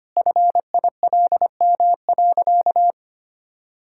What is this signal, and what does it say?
Morse code, heard as a keyed tone, about 25 wpm, 700 Hz. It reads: FILM.
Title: Sentences from Top 1000 Words - Review Part 5 - 25wpm